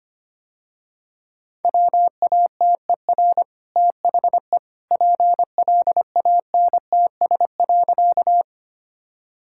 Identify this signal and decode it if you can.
Morse code, heard as a keyed tone, about 25 wpm, 690 Hz. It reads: WATER THE PLANTS.
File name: Easy Sentences with Hints - 25wpm